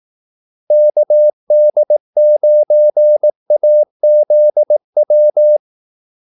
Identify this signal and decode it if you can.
Morse code, heard as a keyed tone, about 18 wpm, 600 Hz. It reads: KD9AZW
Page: Kaldesignaler 18 wpm